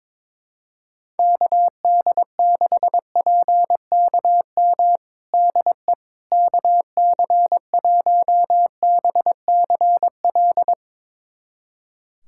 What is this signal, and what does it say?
Morse code, heard as a keyed tone, about 21 wpm, 700 Hz. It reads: KD6PKM DE KC1BCL
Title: USA Callsign DE Callsign 22wpm